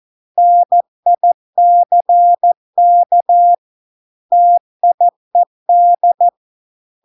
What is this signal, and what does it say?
Morse code, heard as a keyed tone, about 14 wpm, 700 Hz. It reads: NICK TIED